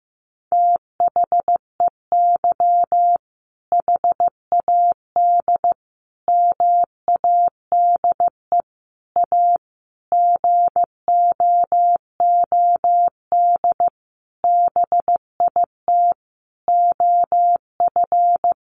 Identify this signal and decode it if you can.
Morse code, heard as a keyed tone, about 15 wpm, 700 Hz. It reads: THEY HAD MADE A GOOD BIT OF
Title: Sentences from Top 900 Words 15wpm